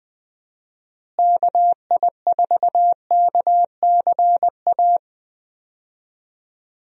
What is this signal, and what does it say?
Morse code, heard as a keyed tone, about 20 wpm, 700 Hz. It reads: KI4KCA
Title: USA Call Signs - 6 Characters - Repeated 3x - 20wpm